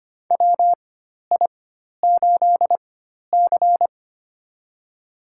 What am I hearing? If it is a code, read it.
Morse code, heard as a keyed tone, about 25 wpm, spaced as for 12 wpm, 700 Hz. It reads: WI8C
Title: USA Call Signs - 4 Characters - 4x Character Spacing - 25wpm